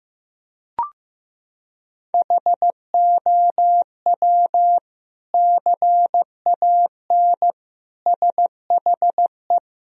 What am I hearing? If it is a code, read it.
Morse code, heard as a keyed tone, about 16 wpm, 700 Hz. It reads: HOW CAN SHE
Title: Sentences from Top 1000 Words - Review Part 4 - 15wpm